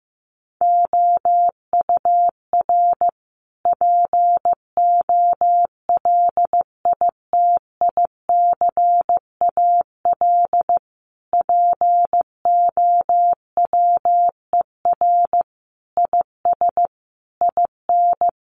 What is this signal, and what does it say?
Morse code, heard as a keyed tone, about 15 wpm, 700 Hz. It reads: OUR POLITICAL POWER IS IN
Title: Sentences from Top 900 Words 15wpm